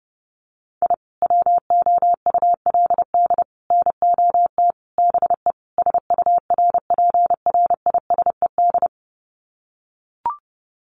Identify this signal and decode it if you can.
Morse code, heard as a keyed tone, about 28 wpm, 700 Hz. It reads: I WOULD NOT BE SURPRISED
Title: Sets of 5 Words Encore2 30wpm